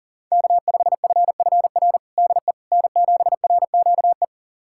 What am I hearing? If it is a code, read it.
Morse code, heard as a keyed tone, about 40 wpm, 700 Hz. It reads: K5FFR DE N7RQE
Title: USA Callsign DE Callsign - 5 Characters - 40wpm